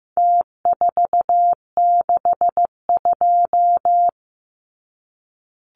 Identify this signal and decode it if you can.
Morse code, heard as a keyed tone, about 15 wpm, 700 Hz. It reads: T462
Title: Numbers - Contest Style - 4-Digit 15wpm